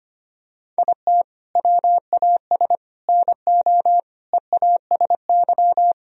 Morse code, heard as a keyed tone, about 25 wpm, 700 Hz. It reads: IT WAS NO EASY
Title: Sentences from Top 1000 Words - Review Part 3 - 25wpm